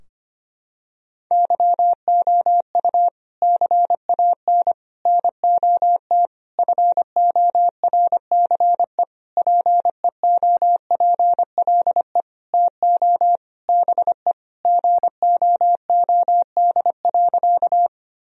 Morse code, heard as a keyed tone, about 25 wpm, 700 Hz. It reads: YOU CAN NOT FORCE PEOPLE TO BE GOOD.
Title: Sentences from Top 1000 Words - Review Part 3 - 25wpm